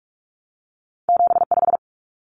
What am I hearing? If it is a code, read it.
Morse code, heard as a keyed tone, about 46 wpm, 700 Hz. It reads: 75
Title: Common Cut Numbers - 2-Digit 45wpm